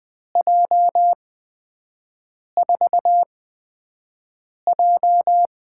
Morse code, heard as a keyed tone, about 20 wpm, spaced as for 5 wpm, 700 Hz. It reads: J4J
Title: USA Call Signs - 6 Characters - 8x Character Spacing - 20wpm